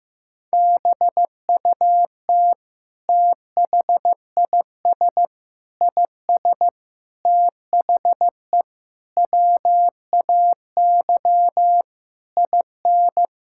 Morse code, heard as a keyed tone, about 15 wpm, 700 Hz. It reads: BUT THIS IS THE WAY IN
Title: Sentences from Top 1000 Words - Review Part 5 - 15wpm